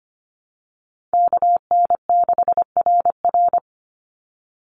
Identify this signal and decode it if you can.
Morse code, heard as a keyed tone, about 25 wpm, 700 Hz. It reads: KN6RR
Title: USA Call Signs 25wpm